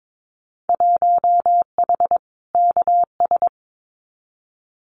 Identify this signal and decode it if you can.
Morse code, heard as a keyed tone, about 22 wpm, 700 Hz. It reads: 1H KS
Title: Winter Field Day Exchanges 22wpm